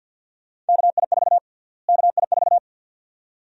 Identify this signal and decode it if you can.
Morse code, heard as a keyed tone, about 52 wpm, 700 Hz. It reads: KI4 KI4
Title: USA Call Sign Prefixes - Repeated 2x - 50wpm